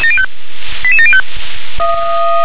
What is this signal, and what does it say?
Five-tone selective call (ZVEI), heard as quick runs of five tones, tones of about 70 ms. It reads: 88705
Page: /SDR/ZVEI/